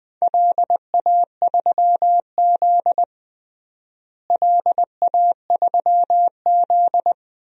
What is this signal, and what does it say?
Morse code, heard as a keyed tone, about 20 wpm, 700 Hz. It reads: LA3Z LA3Z
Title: DXCC-100 Call Signs - 4 Characters - Repeated 3x - 20wpm